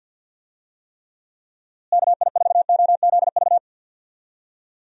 Morse code, heard as a keyed tone, about 51 wpm, 700 Hz. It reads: KI4OZV
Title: USA Call Signs - 6 Characters - Repeated 3x - 50wpm